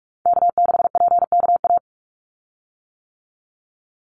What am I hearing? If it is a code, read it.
Morse code, heard as a keyed tone, about 46 wpm, 700 Hz. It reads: K6PKA
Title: USA Call Signs 45wpm